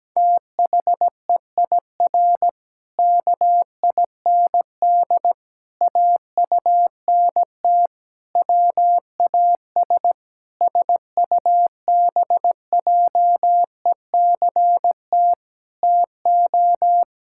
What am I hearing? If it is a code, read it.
Morse code, heard as a keyed tone, about 17 wpm, 700 Hz. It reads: THEIR KIND AUNT WAS SUBJECT TO
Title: Pangram Challenge v2 - 17wpm